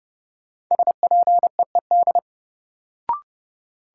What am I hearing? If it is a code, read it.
Morse code, heard as a keyed tone, about 32 wpm, 700 Hz. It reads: SPEED
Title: Sentences from Top 800 Words - New Word Drill - 30wpm